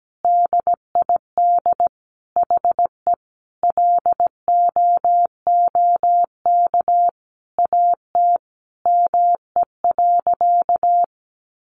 Morse code, heard as a keyed tone, about 17 wpm, 700 Hz. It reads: DID HE LOOK AT ME.
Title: Sentences from Top 1500 Words 17wpm